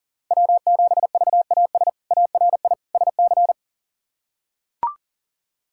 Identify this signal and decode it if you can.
Morse code, heard as a keyed tone, about 40 wpm, 700 Hz. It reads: W7VAS ARI SC
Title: K1USN SST QSO Practice - 40wpm